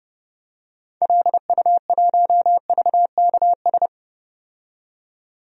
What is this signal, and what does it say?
Morse code, heard as a keyed tone, about 31 wpm, 700 Hz. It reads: LU1VKS
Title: DXCC-100 Call Signs - 6 Characters - 30wpm